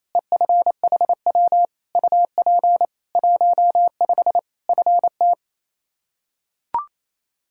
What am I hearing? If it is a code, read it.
Morse code, heard as a keyed tone, about 27 wpm, 700 Hz. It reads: EFHW UP 15 FT
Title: Practice QSO - Exchange 2 - Info - 28wpm